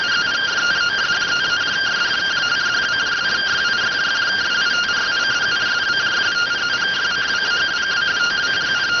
Two-tone teleprinter signal, about 170 Hz shift.